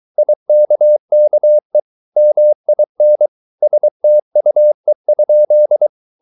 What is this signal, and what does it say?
Morse code, heard as a keyed tone, about 23 wpm, 600 Hz. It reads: IKKE MIN STUE?
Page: Ferie 23 wpm